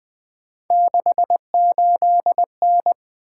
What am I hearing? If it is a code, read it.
Morse code, heard as a keyed tone, about 20 wpm, 700 Hz. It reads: 68N
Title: Three Letter-Number Combinations - 20wpm